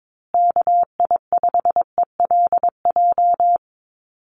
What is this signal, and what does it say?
Morse code, heard as a keyed tone, about 22 wpm, 700 Hz. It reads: KI5ELJ
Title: USA Call Signs Repeated 2x 22wpm